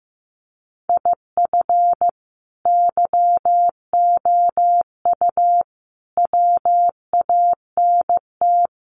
Morse code, heard as a keyed tone, about 15 wpm, 700 Hz. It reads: IF YOU WANT
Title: Sets of 5 Words Encore 15wpm